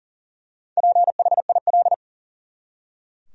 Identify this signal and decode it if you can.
Morse code, heard as a keyed tone, about 39 wpm, 700 Hz. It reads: PHIL
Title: USA Common Names - Max 5 Letters - 40wpm